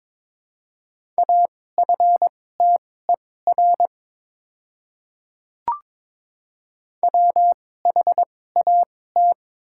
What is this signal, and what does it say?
Morse code, heard as a keyed tone, about 22 wpm, spaced as for 16 wpm, 690 Hz. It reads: AFTER WHAT
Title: 2x Character Spacing - Top 300 Words - 22wpm